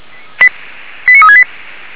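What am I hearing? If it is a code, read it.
Five-tone selective call (ZVEI), heard as quick runs of five tones, tones of about 70 ms. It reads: 89368
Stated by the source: ZVEI five-tone selective call received with an SDR.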